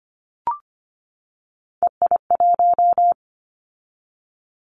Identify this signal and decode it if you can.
Morse code, heard as a keyed tone, about 25 wpm, 700 Hz. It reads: EI1
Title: Callsign Prefixes for Top 50 DXCC - 25wpm